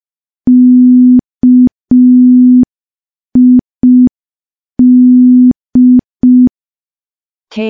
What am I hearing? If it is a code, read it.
Morse code, heard as a keyed tone, about 5 wpm, 250 Hz. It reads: KID
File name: morse / morse-training-05WPM